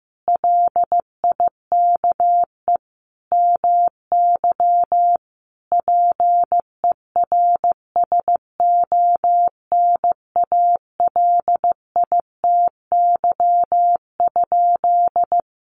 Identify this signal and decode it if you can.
Morse code, heard as a keyed tone, about 15 wpm, 700 Hz. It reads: LIKE MY PERSONALITY?
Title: Sentences from Top 1400 Words 15wpm